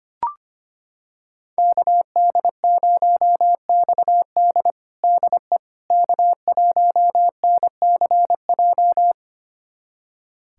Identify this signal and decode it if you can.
Morse code, heard as a keyed tone, about 25 wpm, 700 Hz. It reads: KD0XD DE K1NCJ
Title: USA Callsign DE Callsign - 5 Characters - 25wpm